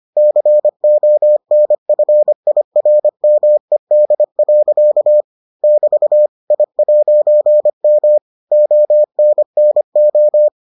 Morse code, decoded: CONFIRMED. <BT> I'M ONNO

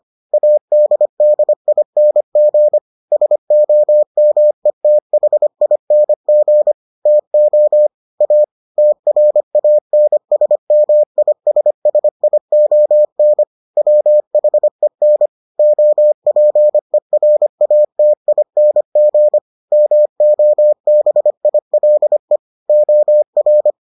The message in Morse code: ADDING SOMETHING TO A TRANSMISSION WHEN OPERATING MOBILE OR

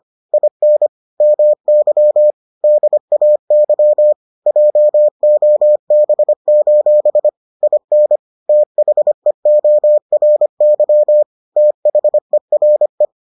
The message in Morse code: IN MY DAY JOB: IN THEORY THERE

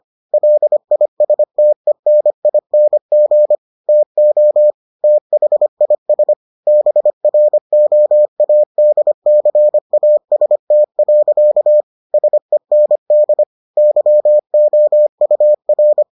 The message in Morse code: LISTENING TO THIS BROADCAST. SEND YOUR

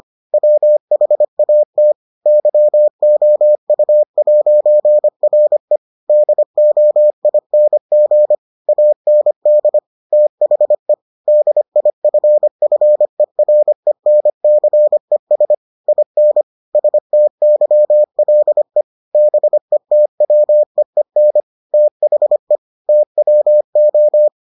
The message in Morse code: WHAT YOU'RE DOING AND THE DIFFERENCES IN STYLE BETWEEN THE TWO

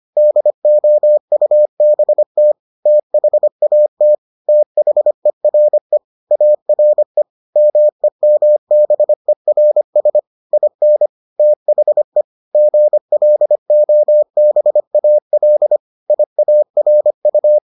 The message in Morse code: DOUBT THAT THERE ARE MEMBERS IN THE GLOBAL IARU

